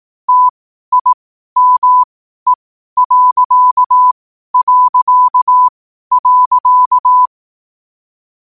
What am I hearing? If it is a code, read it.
Morse code, heard as a keyed tone, about 18 wpm, spaced as for 12 wpm, 1000 Hz. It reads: TIME...
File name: qotdfiles / 12wpm